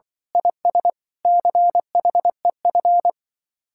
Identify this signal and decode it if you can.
Morse code, heard as a keyed tone, about 23 wpm, 700 Hz. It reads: IS CHEF
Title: Speed-Racing Common QSO Phrases - 20wpm